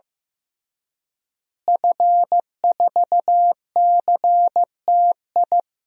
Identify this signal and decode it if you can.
Morse code, heard as a keyed tone, about 15 wpm, 700 Hz. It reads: F4CTI
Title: DXCC-100 Call Signs - 5 Characters - Repeated 3x - 15wpm